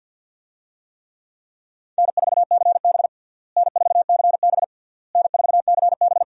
Morse code, heard as a keyed tone, about 52 wpm, 700 Hz. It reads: N4CB N4CB N4CB
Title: USA Call Signs - 4 Characters - Repeated 3x - 50wpm